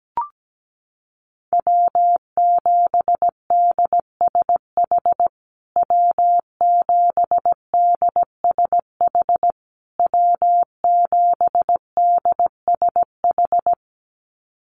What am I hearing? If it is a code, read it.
Morse code, heard as a keyed tone, about 17 wpm, 700 Hz. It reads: W7DSH W7DSH W7DSH
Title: USA Call Signs Repeated 3x 17wpm